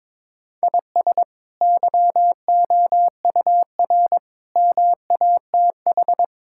Morse code, heard as a keyed tone, about 22 wpm, 700 Hz. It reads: IS YOUR MATH
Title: Sentences from Top 1000 Words - Review Part 6 - 22wpm